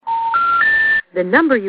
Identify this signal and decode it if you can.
Special information tone, with speech.